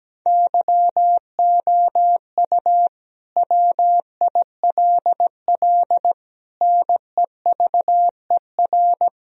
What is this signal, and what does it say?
Morse code, heard as a keyed tone, about 17 wpm, 700 Hz. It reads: YOU WILL NEVER
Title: Sentences from Top 1000 Words - Review Part 2 - 17wpm